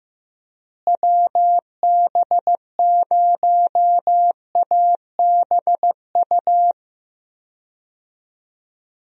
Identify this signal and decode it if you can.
Morse code, heard as a keyed tone, about 15 wpm, 700 Hz. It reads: WB0ABU
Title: USA Call Signs - 6 Characters - Repeated 3x - 15wpm